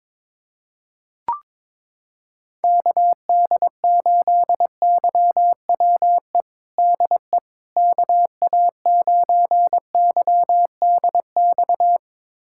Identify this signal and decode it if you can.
Morse code, heard as a keyed tone, about 22 wpm, 700 Hz. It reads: KD8YWE DE KA9YDX
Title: USA Callsign DE Callsign 22wpm